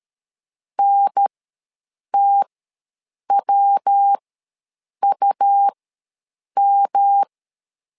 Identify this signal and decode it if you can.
Morse code, heard as a keyed tone, about 13 wpm, spaced as for 7 wpm, 800 Hz. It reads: NTWUM
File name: LEZIONI / 3 MWRO / 3 MW ws 12wpm 1816